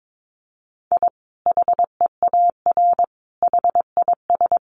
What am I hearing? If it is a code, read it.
Morse code, heard as a keyed tone, about 22 wpm, 700 Hz. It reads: I HEAR HIS
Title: Sentences from Top 500 Words 22wpm